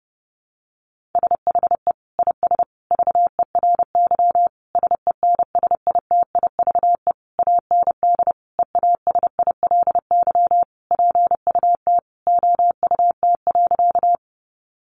Morse code, heard as a keyed tone, about 30 wpm, 700 Hz. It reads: SHE IS VERY SENSITIVE AND EASILY PUT OUT.
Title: Sentences from Top 1400 Words 30wpm